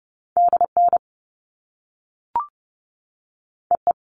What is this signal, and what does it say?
Morse code, heard as a keyed tone, about 30 wpm, 700 Hz. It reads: DN EE
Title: All Cut Numbers - 2-Digit 30wpm